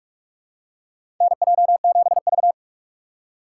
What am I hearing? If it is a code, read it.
Morse code, heard as a keyed tone, about 47 wpm, 700 Hz. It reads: NJ7V
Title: USA Call Signs - 2x Character Spacing - 45wpm